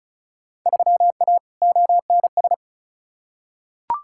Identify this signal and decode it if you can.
Morse code, heard as a keyed tone, about 34 wpm, 700 Hz. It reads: 3A ONS
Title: Field Day Exchanges - 35wpm